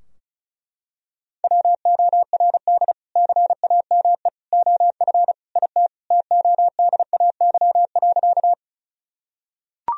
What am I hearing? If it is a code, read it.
Morse code, heard as a keyed tone, about 36 wpm, 700 Hz. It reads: WORD CAME OF IT TODAY.
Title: Sentences from Top 1000 Words - Review Part 4 - 35wpm